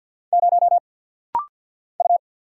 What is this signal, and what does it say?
Morse code, heard as a keyed tone, about 50 wpm, 700 Hz. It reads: 0 U
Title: Single Letter-Number - Rapid-Fire 50wpm